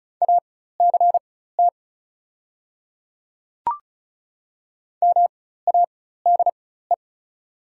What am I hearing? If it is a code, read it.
Morse code, heard as a keyed tone, about 36 wpm, spaced as for 16 wpm, 690 Hz. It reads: ACT MADE
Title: 4x Character Spacing - Top 300 Words - 35wpm